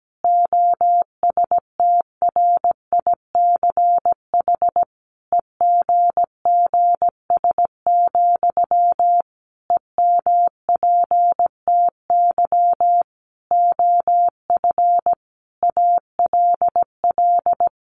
Morse code, heard as a keyed tone, about 17 wpm, 700 Hz. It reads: OSTRICH EGGS, EMPTY OF ALL
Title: A Sound Of Thunder 17wpm